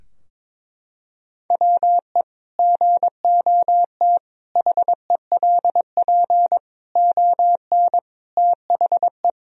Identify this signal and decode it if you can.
Morse code, heard as a keyed tone, about 22 wpm, 700 Hz. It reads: WE GOT HELP ON THE
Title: Sentences from Top 1000 Words - Review Part 2 - 22wpm